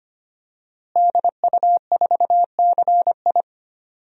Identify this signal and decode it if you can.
Morse code, heard as a keyed tone, about 25 wpm, 700 Hz. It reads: DU4CI
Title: DXCC-100 Call Signs - 5 Characters - Repeated 2x - 25wpm